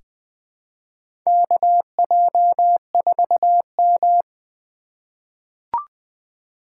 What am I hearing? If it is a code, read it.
Morse code, heard as a keyed tone, about 20 wpm, 700 Hz. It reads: KJ4M
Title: USA Call Signs - 4 Characters - 4x Character Spacing - 20wpm